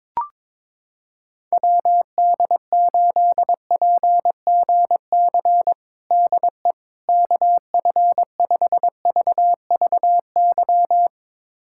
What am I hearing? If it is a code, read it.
Morse code, heard as a keyed tone, about 22 wpm, 700 Hz. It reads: WD8PGC DE KF5VVY
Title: USA Callsign DE Callsign - 6 Characters - 22wpm